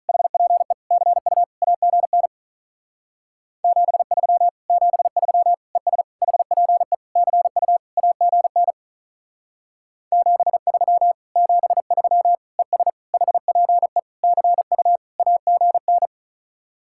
Morse code, heard as a keyed tone, about 44 wpm, 700 Hz. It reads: HPE CU AGN 73 73 ES HPE CU AGN 73 73 ES HPE CU AGN